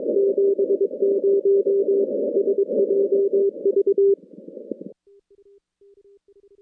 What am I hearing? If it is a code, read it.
Morse code, heard as a keyed tone, about 22 wpm, 400 Hz. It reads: 70 SJV
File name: contests / CW-SS-k5tr- -2200